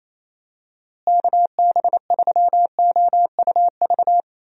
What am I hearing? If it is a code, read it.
Morse code, heard as a keyed tone, about 28 wpm, 700 Hz. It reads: KB3OUV
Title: USA Call Signs - 6 Characters - Repeated 2x - 28wpm